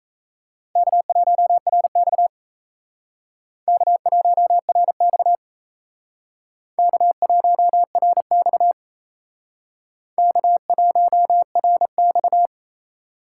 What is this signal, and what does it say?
Morse code, heard as a keyed tone, about 35 wpm, 700 Hz. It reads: K1RX K1RX K1RX K1RX
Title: Speed Racing - USA 1x2 Call Signs 28wpm